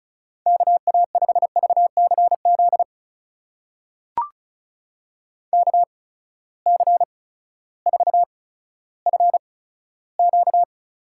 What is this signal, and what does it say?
Morse code, heard as a keyed tone, about 36 wpm, spaced as for 16 wpm, 700 Hz. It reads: KA5VCZ KC4FQ
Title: USA Call Signs - 6 Characters - 8x Character Spacing - 35wpm